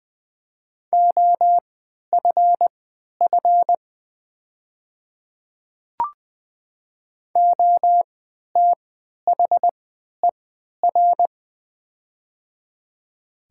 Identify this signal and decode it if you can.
Morse code, heard as a keyed tone, about 20 wpm, spaced as for 11 wpm, 700 Hz. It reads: OFF OTHER
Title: 3x Character Spacing - Top 200 Words - 20wpm